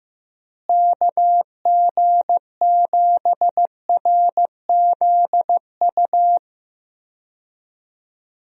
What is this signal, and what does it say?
Morse code, heard as a keyed tone, about 15 wpm, 700 Hz. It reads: KG7RZU
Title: USA Call Signs - 6 Characters - Repeated 3x - 15wpm